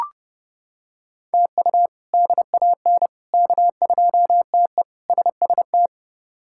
Morse code, heard as a keyed tone, about 32 wpm, 700 Hz. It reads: TU DAN K2TE SST
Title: K1USN SST QSO Practice - 30wpm